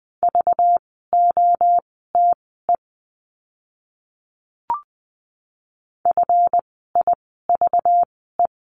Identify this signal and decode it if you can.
Morse code, heard as a keyed tone, about 20 wpm, spaced as for 14 wpm, 700 Hz. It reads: VOTE FIVE
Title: 192 Lesson 31f - Letter v - Words 20wpm